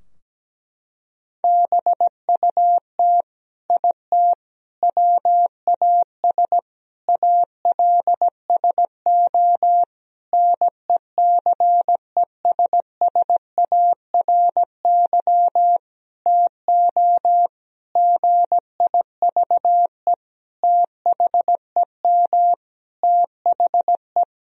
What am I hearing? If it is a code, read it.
Morse code, heard as a keyed tone, about 17 wpm, 700 Hz. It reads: BUT IT WAS ALSO NECESSARY TO GIVE THEM THE
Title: Sentences from Top 1000 Words - Review Part 2 - 17wpm